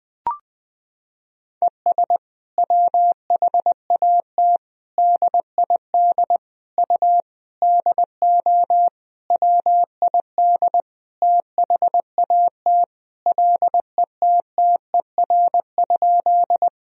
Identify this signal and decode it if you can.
Morse code, heard as a keyed tone, about 20 wpm, 700 Hz. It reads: ES WHAT DID U DO WID THAT LETTER?
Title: Sentences from Top 300 Words and Abbreviations 20wpm